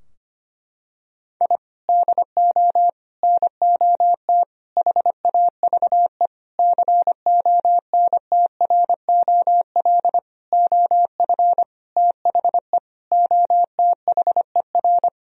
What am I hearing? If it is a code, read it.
Morse code, heard as a keyed tone, about 25 wpm, 700 Hz. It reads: I DO NOT HAVE CONTROL OF THE OTHER
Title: Sentences from Top 1000 Words - Review Part 3 - 25wpm